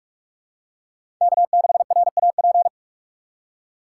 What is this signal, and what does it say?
Morse code, heard as a keyed tone, about 46 wpm, 700 Hz. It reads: K6RAP
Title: USA Call Signs - 5 Characters - 2x Character Spacing - 45wpm